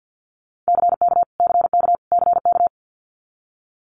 Morse code, heard as a keyed tone, about 50 wpm, 700 Hz. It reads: CK CK CK
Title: Multi-letter Phonemes Repeated 3x 50wpm